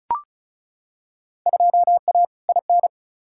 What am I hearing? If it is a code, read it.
Morse code, heard as a keyed tone, about 36 wpm, 700 Hz. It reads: 2A IN
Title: Field Day Exchanges - 35wpm